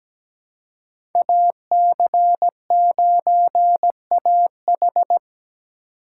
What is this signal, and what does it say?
Morse code, heard as a keyed tone, about 17 wpm, 700 Hz. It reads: AC9AH
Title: USA Call Signs - 4x Character Spacing - 17wpm